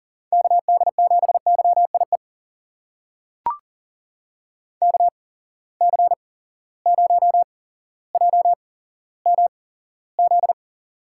Morse code, heard as a keyed tone, about 39 wpm, spaced as for 17 wpm, 700 Hz. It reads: KD7YIE KC0JMZ
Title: USA Call Signs - 6 Characters - 8x Character Spacing - 40wpm